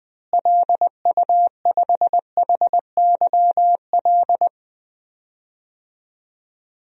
Morse code, heard as a keyed tone, about 19 wpm, 700 Hz. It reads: LU5HYL
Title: DXCC-100 Call Signs - 6 Characters - Repeated 2x - 20wpm